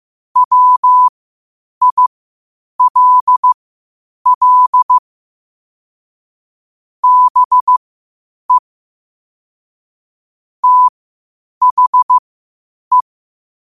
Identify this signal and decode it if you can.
Morse code, heard as a keyed tone, about 15 wpm, spaced as for 8 wpm, 1000 Hz. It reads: WILL BE THE
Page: 8 wpm morse code quotes